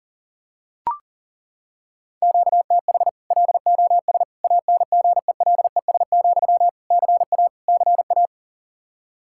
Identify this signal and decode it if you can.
Morse code, heard as a keyed tone, about 40 wpm, 700 Hz. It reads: QTH LOS ANGELES, CA CA
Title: Practice QSO - Exchange 1 - Beginning - 40wpm